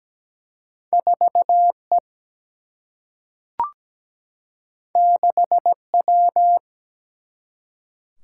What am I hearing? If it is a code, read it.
Morse code, heard as a keyed tone, about 17 wpm, 700 Hz. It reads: 4E 6W
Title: Single-Letter Single-Number 17wpm